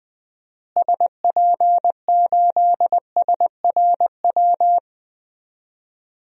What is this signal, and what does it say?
Morse code, heard as a keyed tone, about 20 wpm, 700 Hz. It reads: SP8SRW
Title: DXCC-100 Call Signs - 6 Characters - Repeated 3x - 20wpm